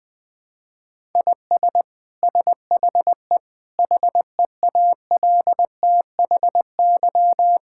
Morse code, heard as a keyed tone, about 20 wpm, 700 Hz. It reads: IS SHE HEALTHY
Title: Sentences from Top 1000 Words - Review Part 5 - 20wpm